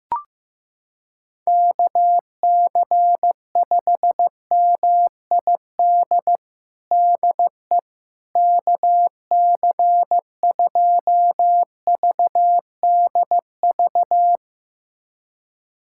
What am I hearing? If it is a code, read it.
Morse code, heard as a keyed tone, about 15 wpm, 700 Hz. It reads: KC5MID DE KC2VDV
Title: USA Callsign DE Callsign - 6 Characters - 15wpm